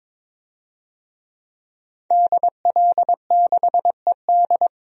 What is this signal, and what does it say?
Morse code, heard as a keyed tone, about 22 wpm, 700 Hz. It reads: DL6ED
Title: DXCC-100 Call Signs - 5 Characters - Repeated 2x - 22wpm